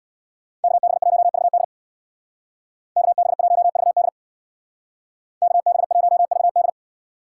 Morse code, heard as a keyed tone, about 60 wpm, 700 Hz. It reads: KB1VD KB1VD KB1VD
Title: Speed Racing - USA 2x2 Call Signs 50wpm